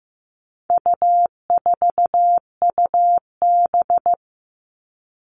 Morse code, heard as a keyed tone, about 15 wpm, 700 Hz. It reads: U4UB
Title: All Cut Numbers - 4-Digit 15wpm